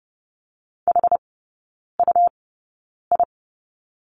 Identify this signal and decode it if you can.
Morse code, heard as a keyed tone, about 30 wpm, spaced as for 9 wpm, 700 Hz. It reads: HUI
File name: Lesson 19e - Letter w - Three Characters 30wpm